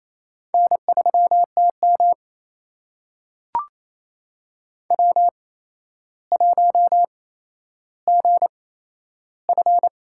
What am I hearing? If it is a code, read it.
Morse code, heard as a keyed tone, about 28 wpm, spaced as for 12 wpm, 700 Hz. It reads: N3TM W1GF
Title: USA Call Signs - 4 Characters - 8x Character Spacing - 28wpm